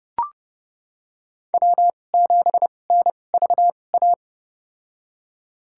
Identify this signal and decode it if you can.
Morse code, heard as a keyed tone, about 30 wpm, spaced as for 22 wpm, 700 Hz. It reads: W7NVA
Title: USA Call Signs - 5 Characters - 2x Character Spacing - 30wpm